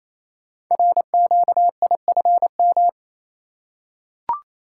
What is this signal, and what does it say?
Morse code, heard as a keyed tone, about 27 wpm, 700 Hz. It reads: RQIFM
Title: Code Groups - 5 Letters - 28wpm